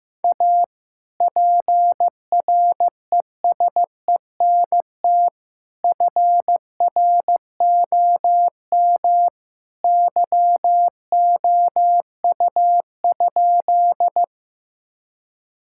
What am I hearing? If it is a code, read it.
Morse code, heard as a keyed tone, about 15 wpm, 700 Hz. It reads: A PRESENT FROM YOU?
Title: Sentences from Top 1000 Words - Review Part 5 - 15wpm